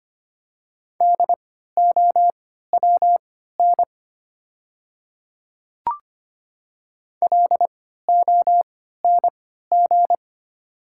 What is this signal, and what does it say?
Morse code, heard as a keyed tone, about 25 wpm, spaced as for 14 wpm, 700 Hz. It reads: DOWN LONG
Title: 3x Character Spacing - Top 100 Words - 25wpm